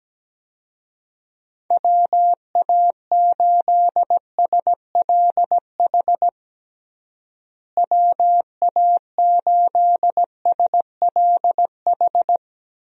Morse code, heard as a keyed tone, about 17 wpm, 700 Hz. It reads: WA8SLH WA8SLH
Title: USA Call Signs - 6 Characters - Repeated 3x - 17wpm